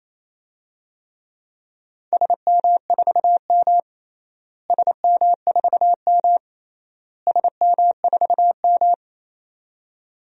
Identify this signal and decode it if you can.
Morse code, heard as a keyed tone, about 27 wpm, 700 Hz. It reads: SM4M SM4M SM4M
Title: DXCC-100 Call Signs - 4 Characters - Repeated 3x - 28wpm